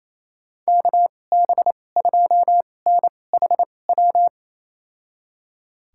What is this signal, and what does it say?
Morse code, heard as a keyed tone, about 28 wpm, spaced as for 20 wpm, 700 Hz. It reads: KB2NHW
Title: USA Call Signs - 6 Characters - 2x Character Spacing - 28wpm